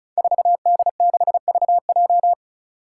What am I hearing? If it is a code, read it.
Morse code, heard as a keyed tone, about 34 wpm, 700 Hz. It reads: 4D6VJ
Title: Five Challenging Letter-Number Combinations 35wpm